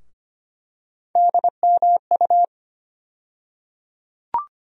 Morse code, heard as a keyed tone, about 25 wpm, 700 Hz. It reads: DMU
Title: USA Call Sign Postfixes - 3 Letters 25wpm